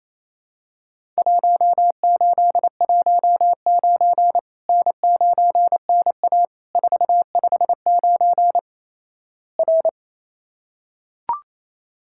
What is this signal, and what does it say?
Morse code, heard as a keyed tone, about 28 wpm, 700 Hz. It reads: 1819 N9NA 459 R
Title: Worked All Europe DX Contest - QTC Exchange - 28wpm